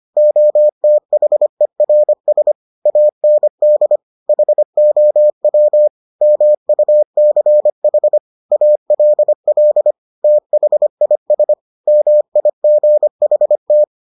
Morse code, heard as a keyed tone, about 25 wpm, 600 Hz. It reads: OTHERS AND HOW MUCH ALL THIS MIGHT